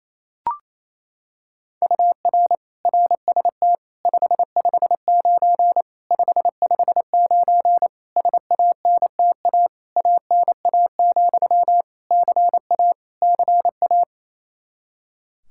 Morse code, heard as a keyed tone, about 28 wpm, 700 Hz. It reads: UR RST 559 559 SANTA ANA, CA CA
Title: Practice QSO - Exchange 1 - Beginning - 28wpm